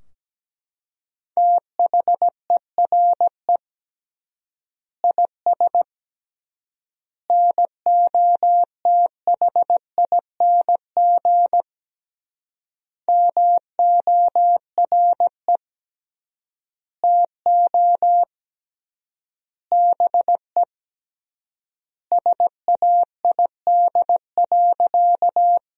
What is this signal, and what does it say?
Morse code, heard as a keyed tone, about 17 wpm, 700 Hz. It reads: THERE IS NOTHING MORE TO BE SAID.
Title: 3x Word-Spacing - Sentences from Top 500 Words - 17wpm